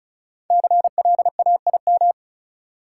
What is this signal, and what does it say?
Morse code, heard as a keyed tone, about 36 wpm, 700 Hz. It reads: CLAIM
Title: Sentences from Top 1500 Words - New Word Drill Alt - 35wpm